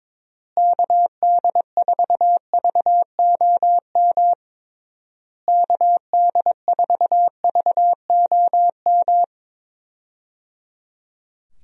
Morse code, heard as a keyed tone, about 22 wpm, 700 Hz. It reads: KD4VOM KD4VOM
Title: USA Call Signs - 6 Characters - Repeated 3x - 22wpm